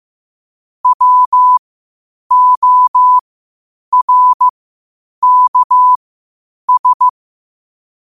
Morse code, heard as a keyed tone, about 15 wpm, spaced as for 8 wpm, 1000 Hz. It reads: WORKS
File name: qotdfiles / 8wpm